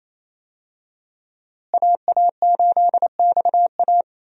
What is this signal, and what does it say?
Morse code, heard as a keyed tone, about 28 wpm, 700 Hz. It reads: AA8XA
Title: USA Call Signs - 5 Characters - Repeated 3x - 28wpm